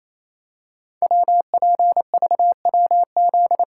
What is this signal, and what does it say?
Morse code, heard as a keyed tone, about 27 wpm, 700 Hz. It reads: WPVWZ
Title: Code Groups - 5 Letters - 28wpm